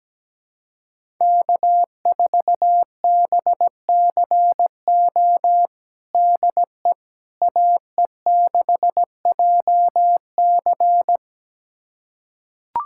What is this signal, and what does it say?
Morse code, heard as a keyed tone, about 17 wpm, 700 Hz. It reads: K4BCO DE AE6JC
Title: USA Callsign DE Callsign - 5 Characters - 17wpm